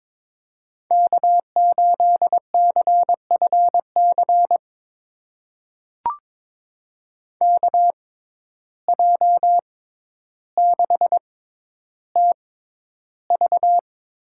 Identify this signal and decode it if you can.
Morse code, heard as a keyed tone, about 22 wpm, spaced as for 11 wpm, 700 Hz. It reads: K8CFC KJ6TV
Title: USA Call Signs - 6x Character Spacing - 22wpm